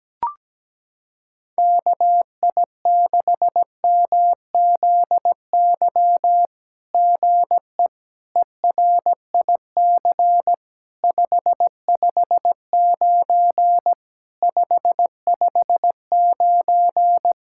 Morse code, heard as a keyed tone, about 18 wpm, 700 Hz. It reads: KI6MZY GE ERIC 559 559
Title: SOTA QSO Practice 17wpm